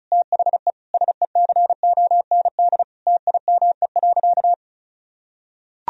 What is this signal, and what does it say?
Morse code, heard as a keyed tone, about 36 wpm, 700 Hz. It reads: THE SECOND TIME.
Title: Sentences from Top 1000 Words - Review Part 1 - 35wpm